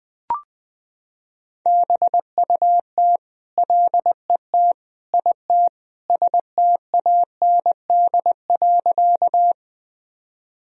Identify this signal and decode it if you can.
Morse code, heard as a keyed tone, about 20 wpm, 700 Hz. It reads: BUT LET IT STAND.
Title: Sentences from Top 1000 Words - Review Part 1 - 20wpm